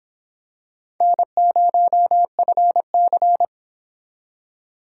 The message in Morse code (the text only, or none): N0FC